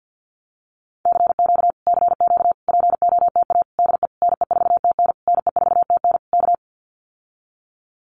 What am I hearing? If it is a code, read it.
Morse code, heard as a keyed tone, about 50 wpm, 700 Hz. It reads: CQ CQ POTA DE NE4TN NE4TN K
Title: Common QSO Phrases Encore 50wpm